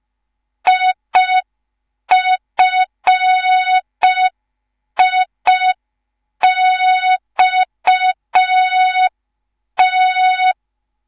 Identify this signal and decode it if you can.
Morse code, heard as a keyed tone, about 5 wpm, 3700 Hz. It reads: IFIXT
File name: Morze-HA2MN / hang / 10GWZ HA2MN